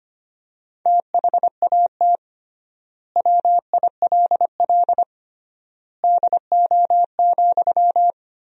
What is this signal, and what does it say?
Morse code, heard as a keyed tone, about 25 wpm, 700 Hz. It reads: THAT WILL DO,
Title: 3x Word-Spacing - Sentences from Top 100 Words - 25wpm